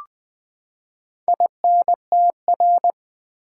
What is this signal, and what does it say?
Morse code, heard as a keyed tone, about 20 wpm, 700 Hz. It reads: INTR
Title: Build A Word - 4 Letters At A Time - 12-Letter Words - 20wpm